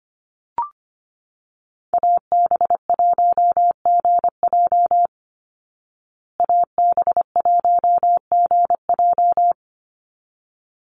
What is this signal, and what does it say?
Morse code, heard as a keyed tone, about 25 wpm, 700 Hz. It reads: AB1GJ AB1GJ
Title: Call Signs Repeated 3x - 25wpm